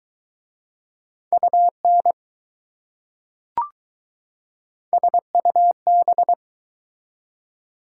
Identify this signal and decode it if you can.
Morse code, heard as a keyed tone, about 20 wpm, 700 Hz. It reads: UN SUB